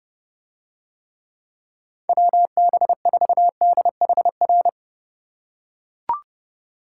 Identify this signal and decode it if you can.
Morse code, heard as a keyed tone, about 30 wpm, 700 Hz. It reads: WB4DHR